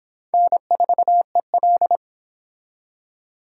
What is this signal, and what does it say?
Morse code, heard as a keyed tone, about 24 wpm, 700 Hz. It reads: N4EL